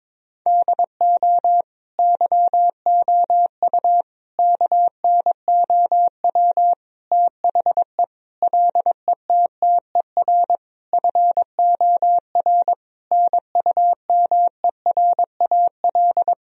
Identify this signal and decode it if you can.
Morse code, heard as a keyed tone, about 22 wpm, 700 Hz. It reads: DO YOU KNOW THE LETTER FOR NUMERAL
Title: Sentences from Top 1000 Words - Review Part 2 - 22wpm